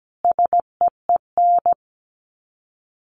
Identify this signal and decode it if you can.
Morse code, heard as a keyed tone, about 17 wpm, 700 Hz. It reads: SEEN
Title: Sets of 5 Words Encore 17wpm